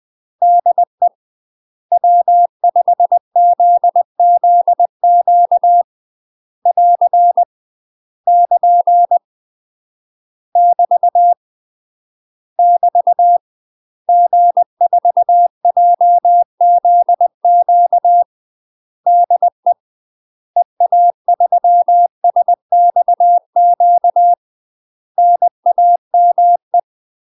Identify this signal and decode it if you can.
Morse code, heard as a keyed tone, about 20 wpm, 700 Hz. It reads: DE W5ZZQ <AR> <KN> <BT> <BT> G4JZQ DE EA3SXQ NAME
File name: UK MASTER QSOs TRAINING / 20WPM / UK MASTER QSOs 0030